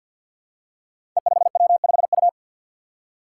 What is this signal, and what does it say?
Morse code, heard as a keyed tone, about 51 wpm, 700 Hz. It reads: E5W5U
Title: Five Challenging Letter-Number Combinations 50wpm